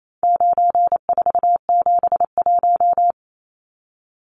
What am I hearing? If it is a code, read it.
Morse code, heard as a keyed tone, about 28 wpm, 700 Hz. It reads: 9471
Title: Numbers - 4-Digit 28wpm